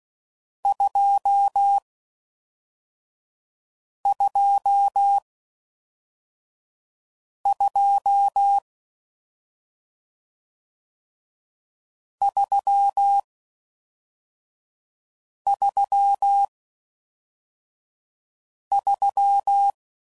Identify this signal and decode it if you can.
Morse code, heard as a keyed tone, about 16 wpm, 800 Hz. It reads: 2 2 2 3 3 3